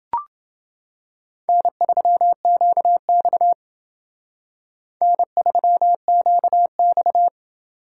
Morse code, heard as a keyed tone, about 28 wpm, 700 Hz. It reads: N3QX N3QX